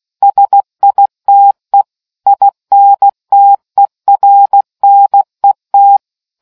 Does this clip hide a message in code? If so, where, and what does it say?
Yes: Morse code from 0.2 to 6.0 s, heard as a keyed tone, about 16 wpm, 800 Hz. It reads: SITE INTERNET